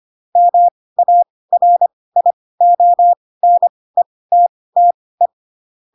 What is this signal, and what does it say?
Morse code, heard as a keyed tone, about 25 wpm, spaced as for 18 wpm, 700 Hz. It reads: MARIONETTE
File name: Pinocchio ch3 12 25 ik1hge